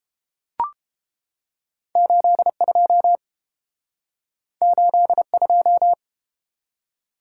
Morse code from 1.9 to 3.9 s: 82

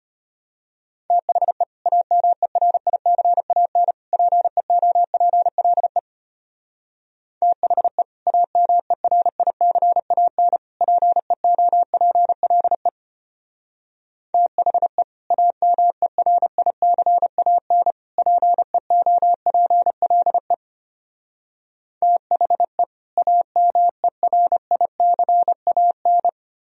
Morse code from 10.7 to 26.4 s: PEOPLE THE AMERICAN PEOPLE THE AMERICAN

